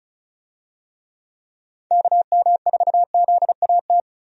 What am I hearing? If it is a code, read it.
Morse code, heard as a keyed tone, about 34 wpm, 700 Hz. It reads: KM4ZAT
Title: USA Call Signs - 6 Characters - Repeated 3x - 35wpm